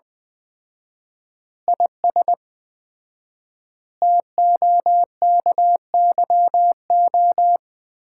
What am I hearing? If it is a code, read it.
Morse code, heard as a keyed tone, about 21 wpm, 700 Hz. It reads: IS TOKYO
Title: Common QSO Phrases - 4x Word-Spacing - 20wpm